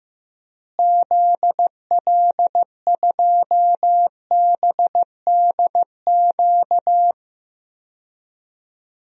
Morse code, heard as a keyed tone, about 15 wpm, 700 Hz. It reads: ZL2BDQ
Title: DXCC-100 Call Signs - 6 Characters - Repeated 2x - 15wpm